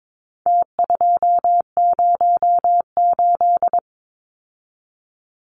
Morse code, heard as a keyed tone, about 22 wpm, 700 Hz. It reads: T208
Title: Numbers - Contest Style - 4-Digit 22wpm